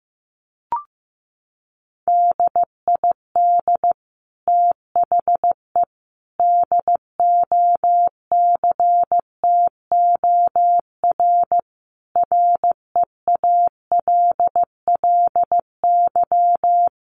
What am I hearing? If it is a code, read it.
Morse code, heard as a keyed tone, about 15 wpm, 700 Hz. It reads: DID THE DOCTOR REALLY
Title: Sentences from Top 1500 Words 15wpm